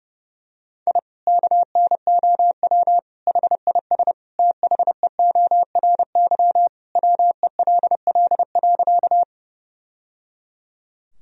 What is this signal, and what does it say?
Morse code, heard as a keyed tone, about 30 wpm, 700 Hz. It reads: I KNOW HIS THEORY WELL.
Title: Sentences from Top 1000 Words - Review Part 3 - 30wpm